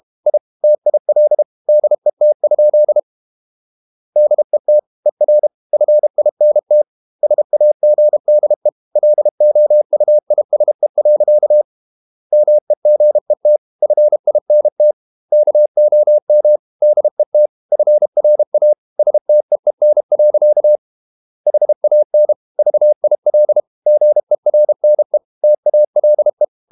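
Morse code, heard as a keyed tone, about 32 wpm, 600 Hz. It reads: I TIL DET? DET ER FINT SAGDE LOUISE. MEGET FINT KOM DET FRA STEEN. HAN VIL GERNE TALE
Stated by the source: High-speed QRQ CW at 32 WPM.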